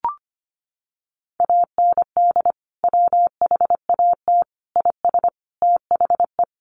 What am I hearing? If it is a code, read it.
Morse code, heard as a keyed tone, about 25 wpm, 700 Hz. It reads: AND WHAT IS THE
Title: Sentences from Top 800 Words 25wpm